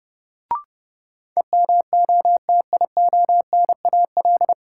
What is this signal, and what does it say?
Morse code, heard as a keyed tone, about 30 wpm, 700 Hz. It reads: EMOTIONAL
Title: Sentences from Top 1000 Words - Review Part 5 - New Word Drill - 30wpm